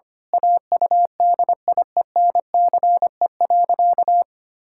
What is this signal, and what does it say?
Morse code, heard as a keyed tone, about 25 wpm, 700 Hz. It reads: AUDIENCE.
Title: Pangram Challenge v2 - 25wpm